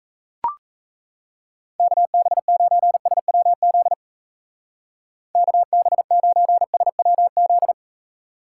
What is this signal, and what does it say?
Morse code, heard as a keyed tone, about 34 wpm, 700 Hz. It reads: KB9SWZ KB9SWZ
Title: Speed Racing - USA Call Signs 28wpm